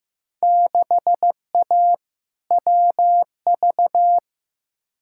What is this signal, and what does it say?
Morse code, heard as a keyed tone, about 15 wpm, 700 Hz. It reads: 6A WV
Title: Field Day Exchanges - 15wpm